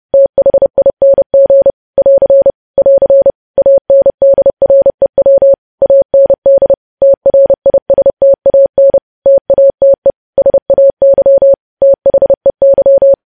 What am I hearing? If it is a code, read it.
Morse code, heard as a keyed tone, about 30 wpm, 550 Hz. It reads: THING <AR> <AR> ANDREW AND TRISTAN TATE SAY THEY
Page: news headlines in morse code at 30 wpm